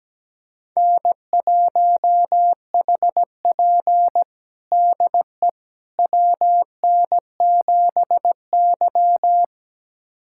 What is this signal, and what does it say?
Morse code, heard as a keyed tone, about 17 wpm, 700 Hz. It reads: N1HP DE WN7Y
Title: USA Callsign DE Callsign - 4 Characters - 17wpm